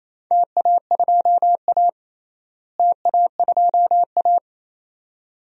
Morse code, heard as a keyed tone, about 28 wpm, 700 Hz. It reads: TA2A TA2A
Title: DXCC-100 Call Signs - 4 Characters - Repeated 2x - 28wpm